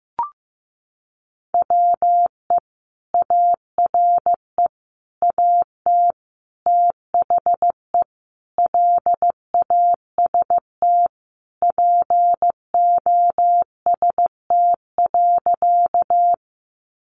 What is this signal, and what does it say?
Morse code, heard as a keyed tone, about 15 wpm, 700 Hz. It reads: WE ARE AT THE LAST POST.
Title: Sentences from Top 1000 Words 15wpm